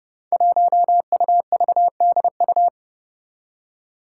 Morse code, heard as a keyed tone, about 30 wpm, 700 Hz. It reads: 1UVDU
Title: Five Challenging Letter-Number Combinations 30wpm